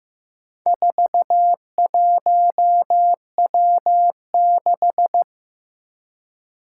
Morse code, heard as a keyed tone, about 15 wpm, 700 Hz. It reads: 41W6
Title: Four Challenging Letter-Number Combinations 15wpm